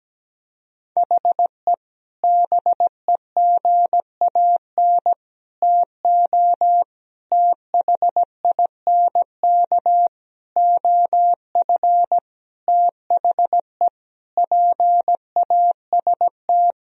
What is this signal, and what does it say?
Morse code, heard as a keyed tone, about 17 wpm, 700 Hz. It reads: HE BEGAN TO THINK OF THE PAST
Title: Sentences from Top 1000 Words - Review Part 3 - 17wpm